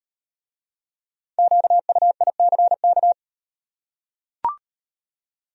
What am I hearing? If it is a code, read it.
Morse code, heard as a keyed tone, about 38 wpm, 700 Hz. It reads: QUICK